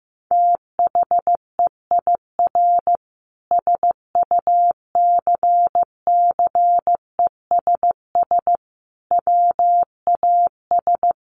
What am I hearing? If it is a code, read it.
Morse code, heard as a keyed tone, about 15 wpm, 700 Hz. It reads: THEIR SUCCESS WAS
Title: Sentences from Top 1500 Words 15wpm